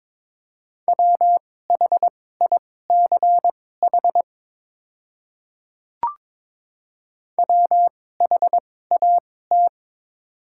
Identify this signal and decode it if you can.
Morse code, heard as a keyed tone, about 22 wpm, spaced as for 16 wpm, 700 Hz. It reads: WHICH WHAT
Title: 2x Character Spacing - Top 100 Words - 22wpm